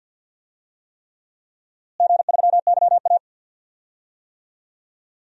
Morse code, heard as a keyed tone, about 53 wpm, 700 Hz. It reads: G3YA
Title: DXCC-100 Call Signs - 4 Characters - Repeated 3x - 50wpm